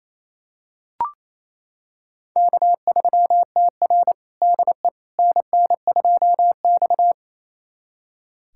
Morse code, heard as a keyed tone, about 29 wpm, 690 Hz. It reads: K3TR DE NN2X
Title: USA Callsign DE Callsign - 4 Characters - 28wpm